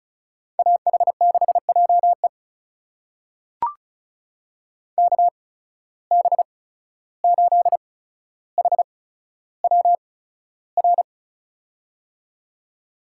Morse code, heard as a keyed tone, about 36 wpm, spaced as for 14 wpm, 700 Hz. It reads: AH6JE KB8HWR
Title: USA Call Signs - 8x Character Spacing - 35wpm